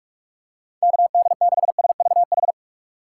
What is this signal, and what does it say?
Morse code, heard as a keyed tone, about 46 wpm, 700 Hz. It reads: KD6SVH
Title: USA Call Signs - 2x Character Spacing - 45wpm